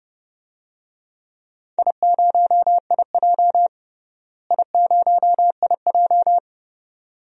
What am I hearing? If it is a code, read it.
Morse code, heard as a keyed tone, about 30 wpm, 700 Hz. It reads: I0IJ I0IJ
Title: DXCC-100 Call Signs - 4 Characters - Repeated 2x - 30wpm